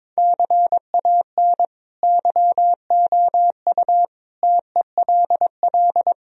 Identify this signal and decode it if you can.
Morse code, heard as a keyed tone, about 22 wpm, 700 Hz. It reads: CAN YOU TELL
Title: Sentences from Top 1000 Words - Review Part 3 - 22wpm